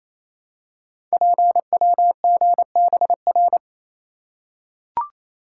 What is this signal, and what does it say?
Morse code, heard as a keyed tone, about 27 wpm, 700 Hz. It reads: PWGBR
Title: Code Groups - 5 Letters - 28wpm